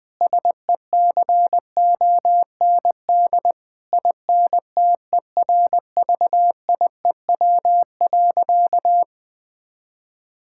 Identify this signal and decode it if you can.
Morse code, heard as a keyed tone, about 20 wpm, 700 Hz. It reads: SECOND INTERVIEW.
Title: Sentences from Top 1600 Words 20wpm